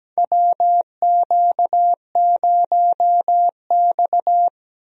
Morse code, heard as a keyed tone, about 17 wpm, 700 Hz. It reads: WQ0X
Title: USA Call Signs - 4 Characters - Repeated 3x - 17wpm